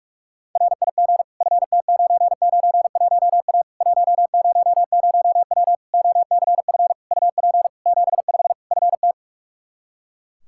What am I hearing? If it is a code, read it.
Morse code, heard as a keyed tone, about 42 wpm, 700 Hz. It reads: RIG FT991A 100W OCF UP 75 FT
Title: K1USN SST QSO Practice 45wpm